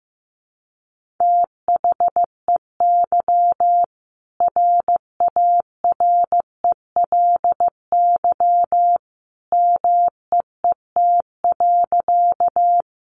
Morse code, heard as a keyed tone, about 15 wpm, 700 Hz. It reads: THEY RARELY MEET.
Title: Sentences from Top 1400 Words 15wpm